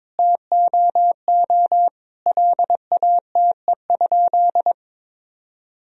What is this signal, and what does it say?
Morse code, heard as a keyed tone, about 21 wpm, 700 Hz. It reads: TOO LATE?
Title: Sentences from Top 300 Words and Abbreviations 22wpm